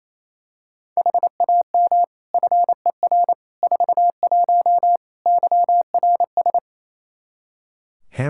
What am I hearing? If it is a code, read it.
Morse code, heard as a keyed tone, about 28 wpm, 700 Hz. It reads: HAM FER 41 YRS
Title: Practice QSO - Exchange 3 - More Info - 28wpm